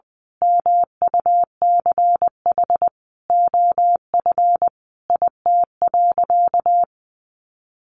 Morse code, heard as a keyed tone, about 20 wpm, 700 Hz. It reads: MUCH OF IT.
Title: Sentences from Top 1400 Words 20wpm